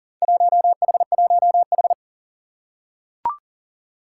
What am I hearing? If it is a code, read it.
Morse code, heard as a keyed tone, about 40 wpm, 700 Hz. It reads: 1H1H
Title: Four Challenging Letter-Number Combinations 40wpm